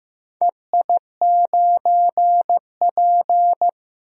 Morse code, heard as a keyed tone, about 15 wpm, 700 Hz. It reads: EI9P
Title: DXCC-100 Call Signs - 4 Characters - Repeated 3x - 15wpm